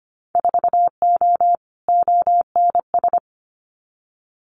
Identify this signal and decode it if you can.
Morse code, heard as a keyed tone, about 25 wpm, 700 Hz. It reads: 4O ONS
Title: Winter Field Day Exchanges 25wpm